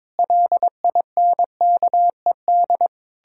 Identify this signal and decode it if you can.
Morse code, heard as a keyed tone, about 22 wpm, 700 Hz. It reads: LINKED
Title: Pangram Challenge v2 - 22wpm